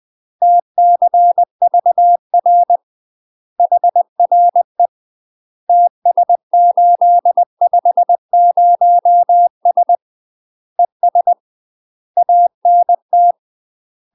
Morse code, heard as a keyed tone, about 20 wpm, 700 Hz. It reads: TCVR HRE TS850S ES ANT
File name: UK MASTER QSOs TRAINING / 20WPM / UK MASTER QSOs 0028